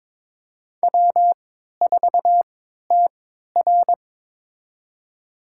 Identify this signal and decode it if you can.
Morse code, heard as a keyed tone, about 22 wpm, spaced as for 13 wpm, 690 Hz. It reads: W4TR
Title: USA Call Signs - 4 Characters - 3x Character Spacing - 22wpm